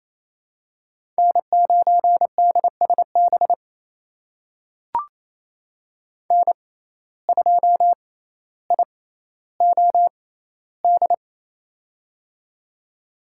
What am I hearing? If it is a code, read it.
Morse code, heard as a keyed tone, about 28 wpm, spaced as for 14 wpm, 700 Hz. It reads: N9DSB N2IOD
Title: USA Call Signs - 5 Characters - 6x Character Spacing - 28wpm